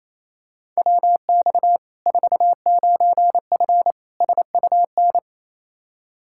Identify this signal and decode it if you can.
Morse code, heard as a keyed tone, about 28 wpm, 700 Hz. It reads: WX 49F SUN
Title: Practice QSO - Exchange 2 - Info - 28wpm